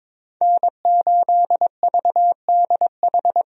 Morse code, heard as a keyed tone, about 22 wpm, 700 Hz. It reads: N8VDH
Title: USA Call Signs - 5 Characters - 22wpm